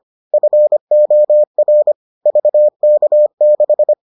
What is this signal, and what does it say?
Morse code, heard as a keyed tone, about 25 wpm, 600 Hz. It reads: FOR VK6